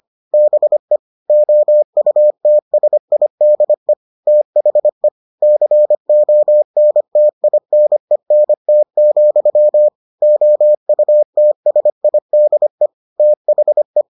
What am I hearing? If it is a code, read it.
Morse code, heard as a keyed tone, about 25 wpm, 600 Hz. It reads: BE OUTSIDE THE CONTINENT, OUTSIDE THE